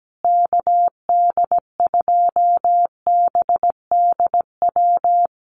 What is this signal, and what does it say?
Morse code, heard as a keyed tone, about 17 wpm, 700 Hz. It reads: KD2BDW
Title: USA Call Signs Repeated 3x 17wpm